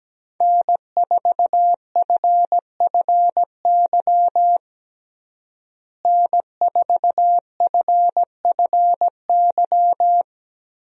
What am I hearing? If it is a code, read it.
Morse code, heard as a keyed tone, about 17 wpm, 700 Hz. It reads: N4FFY N4FFY
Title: USA Call Signs - 5 Characters - Repeated 3x - 17wpm